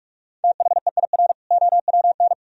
Morse code, heard as a keyed tone, about 44 wpm, 700 Hz. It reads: THEIR OWN